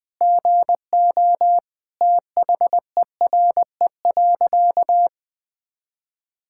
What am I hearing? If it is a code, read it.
Morse code, heard as a keyed tone, about 20 wpm, 700 Hz. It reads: GO THERE.
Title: Sentences with 4-Word Phrases - 20wpm